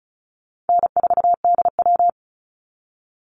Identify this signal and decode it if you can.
Morse code, heard as a keyed tone, about 34 wpm, 700 Hz. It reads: N4DW
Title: Call Signs - 35wpm